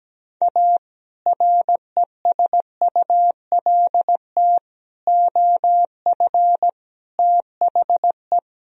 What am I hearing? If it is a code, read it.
Morse code, heard as a keyed tone, about 17 wpm, 700 Hz. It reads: A RESULT OF THE
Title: Speed-Racing - Sets of 4 Words 17wpm